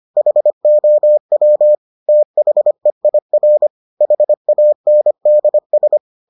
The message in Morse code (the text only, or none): HOW THEIR HANDS